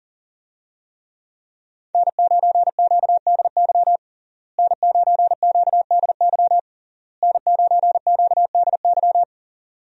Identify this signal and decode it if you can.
Morse code, heard as a keyed tone, about 40 wpm, 700 Hz. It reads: N9QDY N9QDY N9QDY
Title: USA Call Signs - 5 Characters - Repeated 3x - 40wpm